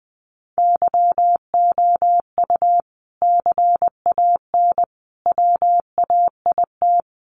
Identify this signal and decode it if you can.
Morse code, heard as a keyed tone, about 20 wpm, 700 Hz. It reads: YOU CAN WAIT
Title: Sentences from Top 1400 Words 20wpm